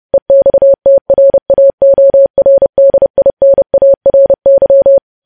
Morse code, heard as a keyed tone, about 30 wpm, 550 Hz. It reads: EXTRAORDINARY
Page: news headlines in morse code at 30 wpm